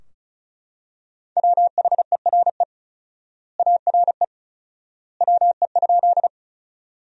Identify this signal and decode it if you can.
Morse code, heard as a keyed tone, about 34 wpm, 700 Hz. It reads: WHERE ARE WE?
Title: 4x Word-Spacing - Sentences from Top 200 Words - 35wpm